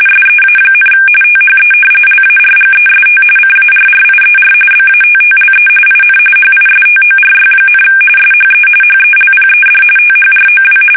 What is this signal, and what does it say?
RTTY, about 45.45 baud. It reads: SEA, THE LARGEST CARRIER BATTLE IN HISTORY, SHE HELPED THROW
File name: fsk850 / battleshipnj